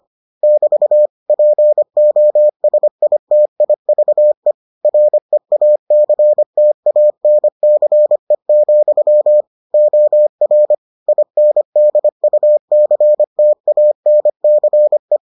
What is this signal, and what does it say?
Morse code, heard as a keyed tone, about 25 wpm, 600 Hz. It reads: <BT> POSITIVE REACTANCE, OR INDUCTANCE